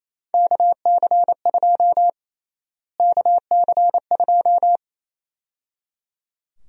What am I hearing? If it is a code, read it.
Morse code, heard as a keyed tone, about 28 wpm, 700 Hz. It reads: KC2 KC2
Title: USA Call Sign Prefixes - Repeated 2x - 28wpm